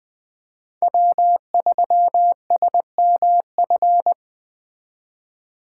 Morse code, heard as a keyed tone, about 21 wpm, 700 Hz. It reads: W3SMF